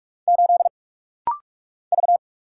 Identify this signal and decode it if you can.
Morse code, heard as a keyed tone, about 44 wpm, 700 Hz. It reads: 8 V
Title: Single Letter-Number Alt - Rapid-Fire 45wpm